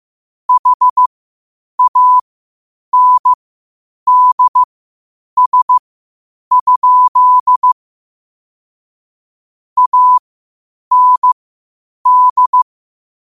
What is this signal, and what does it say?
Morse code, heard as a keyed tone, about 15 wpm, spaced as for 8 wpm, 1000 Hz. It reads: HANDS? AND
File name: qotdfiles / 8wpm